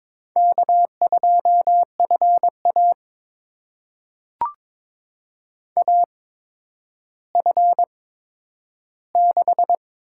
Morse code, heard as a keyed tone, about 22 wpm, spaced as for 11 wpm, 700 Hz. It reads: K2FA AF6
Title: USA Call Signs - 8x Character Spacing - 22wpm